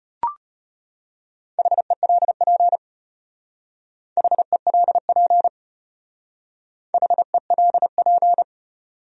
Morse code, heard as a keyed tone, about 31 wpm, 700 Hz. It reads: HELP HELP HELP